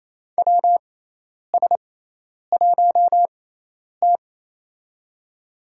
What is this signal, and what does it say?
Morse code, heard as a keyed tone, about 28 wpm, spaced as for 10 wpm, 700 Hz. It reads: WS1T
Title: USA Call Signs - 4 Characters - 6x Character Spacing - 28wpm